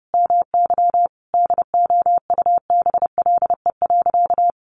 Morse code, heard as a keyed tone, about 30 wpm, 700 Hz. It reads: MY DOUBLE.
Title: Sentences from Top 1400 Words 30wpm